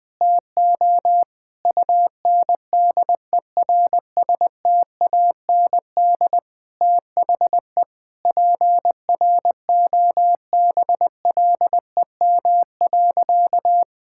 Morse code, heard as a keyed tone, about 20 wpm, 700 Hz. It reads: TO UNDERSTAND THE PROBLEM.
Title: Sentences with 7-Word Phrases - Spoken Beginning - 20wpm